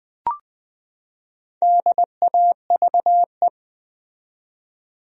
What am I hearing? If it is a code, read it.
Morse code, heard as a keyed tone, about 20 wpm, 700 Hz. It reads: DAVE
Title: USA Common Names - 20wpm